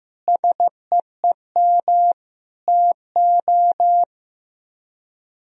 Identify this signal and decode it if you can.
Morse code, heard as a keyed tone, about 15 wpm, 700 Hz. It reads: SEEM TO
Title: Speed-Racing - Sets of 4 Words 15wpm